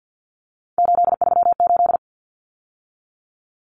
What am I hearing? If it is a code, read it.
Morse code, heard as a keyed tone, about 49 wpm, 700 Hz. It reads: 838
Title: Numbers - 3-Digit 50wpm